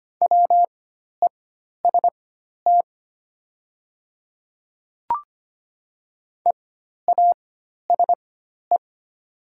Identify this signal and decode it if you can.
Morse code, heard as a keyed tone, about 25 wpm, spaced as for 12 wpm, 690 Hz. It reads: WEST EASE
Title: 4x Character Spacing - Top 500 Words - 25wpm